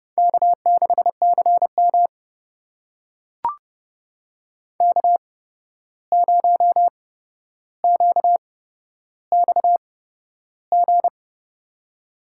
Morse code, heard as a keyed tone, about 30 wpm, spaced as for 12 wpm, 700 Hz. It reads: K6CM K0QXG
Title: USA Call Signs - 8x Character Spacing - 30wpm